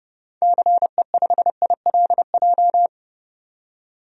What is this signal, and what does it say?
Morse code, heard as a keyed tone, about 30 wpm, 700 Hz. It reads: CE5ILJ
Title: DXCC-100 Call Signs - 6 Characters - 30wpm